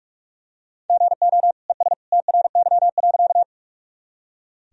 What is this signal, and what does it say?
Morse code, heard as a keyed tone, about 44 wpm, 700 Hz. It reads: GO ES TRY.
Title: Sentences from Top 200 Words and Abbreviations 45wpm